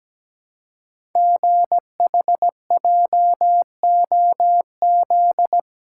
Morse code, heard as a keyed tone, about 17 wpm, 700 Hz. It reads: GHJOZ
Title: Code Groups - 5 Letters - 17wpm